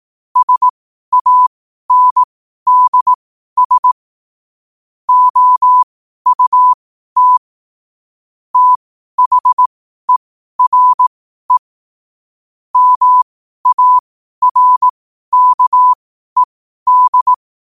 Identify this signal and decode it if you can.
Morse code, heard as a keyed tone, about 18 wpm, spaced as for 12 wpm, 1000 Hz. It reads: SANDS OUT THERE MARKED
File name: qotdfiles / 12wpm